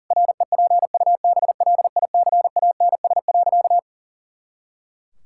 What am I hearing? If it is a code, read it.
Morse code, heard as a keyed tone, about 40 wpm, 700 Hz. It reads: REPUBLICANS.
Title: Pangram Challenge v2 - 40wpm